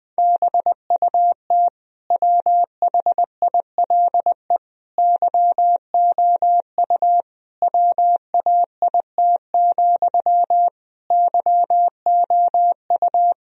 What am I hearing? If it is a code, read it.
Morse code, heard as a keyed tone, about 20 wpm, 700 Hz. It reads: BUT WHILE YOU WAIT, YOU
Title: Sentences from Top 1000 Words - Review Part 1 - 20wpm